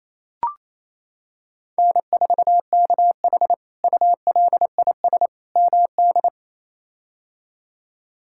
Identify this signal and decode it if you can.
Morse code, heard as a keyed tone, about 28 wpm, 700 Hz. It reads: N4KH ULIS MD
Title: K1USN SST QSO Practice - 28wpm